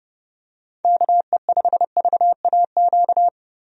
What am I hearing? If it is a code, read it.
Morse code, heard as a keyed tone, about 30 wpm, 700 Hz. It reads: KE5VAQ
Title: USA Call Signs - 6 Characters - 30wpm